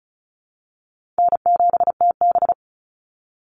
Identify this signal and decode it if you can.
Morse code, heard as a keyed tone, about 34 wpm, 700 Hz. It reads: N7TB
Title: Easy Call Signs Repeated 3x - 35wpm